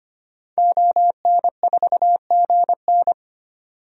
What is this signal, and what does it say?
Morse code, heard as a keyed tone, about 25 wpm, 700 Hz. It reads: ON4GN
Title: DXCC-100 Call Signs - 5 Characters - 25wpm